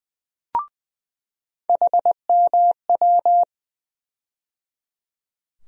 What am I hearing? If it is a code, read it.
Morse code, heard as a keyed tone, about 20 wpm, 700 Hz. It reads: HMW
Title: USA Call Sign Postfixes - 3 Letters 20wpm